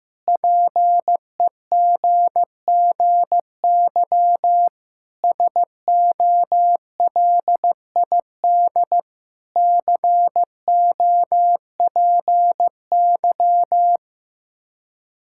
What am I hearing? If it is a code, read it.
Morse code, heard as a keyed tone, about 15 wpm, 700 Hz. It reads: PEGGY SOLID COPY
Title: Practice QSO - Exchange 3 - More Info - 15wpm